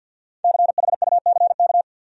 Morse code, heard as a keyed tone, about 47 wpm, 700 Hz. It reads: CHUCK
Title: USA Common Names - Max 5 Letters - 50wpm